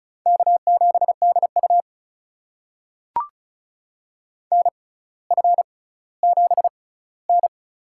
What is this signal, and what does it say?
Morse code, heard as a keyed tone, about 36 wpm, spaced as for 18 wpm, 700 Hz. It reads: K7DU NF7N
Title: USA Call Signs - 4 Characters - 6x Character Spacing - 35wpm